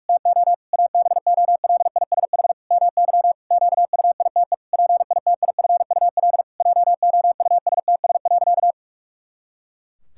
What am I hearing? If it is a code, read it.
Morse code, heard as a keyed tone, about 41 wpm, 700 Hz. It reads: TO ABOLISH MY QUITE PITIFUL JOUSTS.
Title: Pangram Challenge v2 - 45wpm